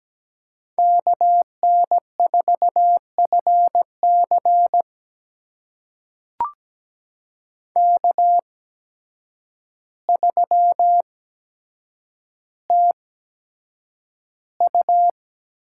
Morse code, heard as a keyed tone, about 17 wpm, spaced as for 8 wpm, 700 Hz. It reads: KN4FC K3TU
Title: USA Call Signs - 8x Character Spacing - 17wpm